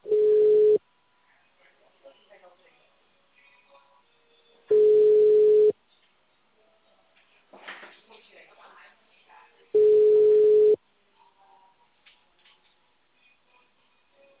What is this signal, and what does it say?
Ringback tone (Europe, 425 Hz), with speech.
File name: out- -2004- - - .38720